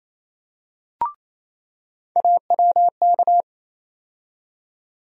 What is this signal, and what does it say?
Morse code, heard as a keyed tone, about 28 wpm, 700 Hz. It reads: AWK
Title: USA Call Sign Postfixes - 3 Letters 28wpm